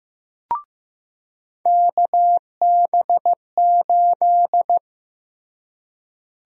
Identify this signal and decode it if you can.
Morse code, heard as a keyed tone, about 16 wpm, 700 Hz. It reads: KB8
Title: USA Call Sign Prefixes - Repeated 2x - 15wpm